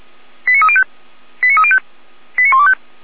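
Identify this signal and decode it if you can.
Five-tone selective call (ZVEI), heard as quick runs of five tones, tones of about 70 ms. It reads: 89305 89305 89125
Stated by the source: ZVEI five-tone selective call received on SDR, Feuerwehr Orlamünde.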